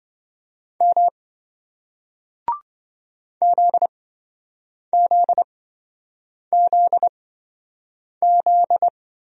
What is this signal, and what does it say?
Morse code, heard as a keyed tone, about 27 wpm, spaced as for 7 wpm, 700 Hz. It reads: M ZZZZ